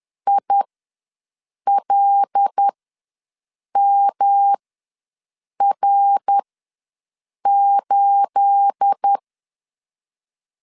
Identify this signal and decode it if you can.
Morse code, heard as a keyed tone, about 10 wpm, spaced as for 6 wpm, 800 Hz. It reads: ILMR8
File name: LEZIONI / 9 28 ws 10wpm 4124